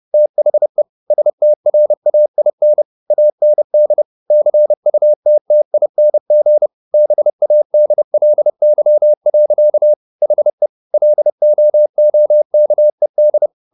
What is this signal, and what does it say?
Morse code, heard as a keyed tone, about 30 wpm, 600 Hz. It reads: THE STRAIN AND CUTTING BADLY. HE LOOKED